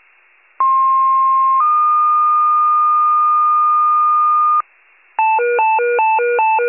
Two-tone page, heard as a one-second tone, then a three-second tone, about 1060 then 1220 Hz, with noise. It is heard from 0.6 to 4.6 s.
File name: wcalert